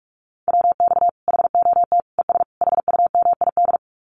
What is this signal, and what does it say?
Morse code, heard as a keyed tone, about 44 wpm, 700 Hz. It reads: WX HOT ES HUMID
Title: Common QSO Phrases 45wpm